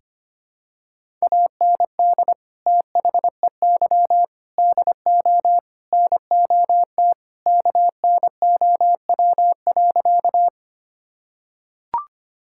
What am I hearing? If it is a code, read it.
Morse code, heard as a keyed tone, about 25 wpm, 700 Hz. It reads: AND THEY DO NOT KNOW.
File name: practice / 25wpm / Sentences with 4-Word Phrases - 25wpm